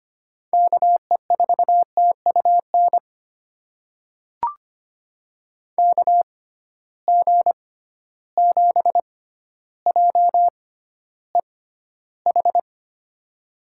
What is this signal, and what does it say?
Morse code, heard as a keyed tone, about 25 wpm, spaced as for 13 wpm, 700 Hz. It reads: KE4TUN KG7JEH
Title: USA Call Signs - 6 Characters - 6x Character Spacing - 25wpm